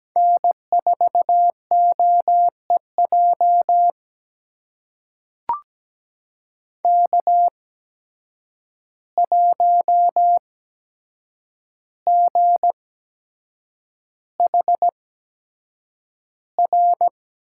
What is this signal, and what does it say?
Morse code, heard as a keyed tone, about 17 wpm, spaced as for 7 wpm, 700 Hz. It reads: N4OEJ K1GHR
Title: USA Call Signs - 5 Characters - 8x Character Spacing - 17wpm